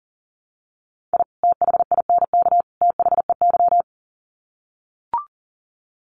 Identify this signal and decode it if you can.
Morse code, heard as a keyed tone, about 37 wpm, 700 Hz. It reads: I THINK THEY
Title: Sets of 3 Words 40wpm